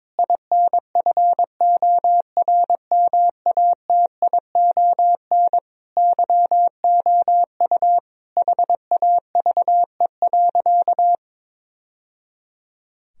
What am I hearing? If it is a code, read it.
Morse code, heard as a keyed tone, about 22 wpm, 700 Hz. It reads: INFORMATION YOU HAVE.
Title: Sentences from Top 1000 Words - Review Part 3 - 22wpm